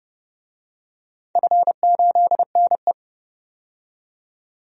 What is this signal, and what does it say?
Morse code, heard as a keyed tone, about 30 wpm, 700 Hz. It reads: F8NE
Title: DXCC-100 Call Signs - 4 Characters - 30wpm